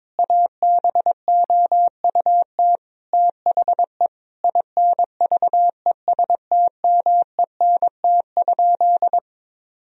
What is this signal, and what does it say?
Morse code, heard as a keyed tone, about 22 wpm, 700 Hz. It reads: ABOUT THE INVESTMENT?
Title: Sentences from Top 1000 Words - Review Part 4 - 22wpm